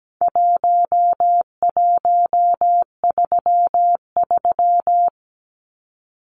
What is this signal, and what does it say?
Morse code, heard as a keyed tone, about 17 wpm, 700 Hz. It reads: 1133
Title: Numbers - 4-Digit 17wpm